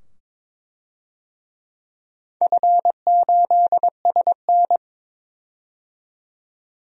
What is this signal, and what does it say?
Morse code, heard as a keyed tone, about 22 wpm, 700 Hz. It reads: F8SN
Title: DXCC-100 Call Signs - 4 Characters - 22wpm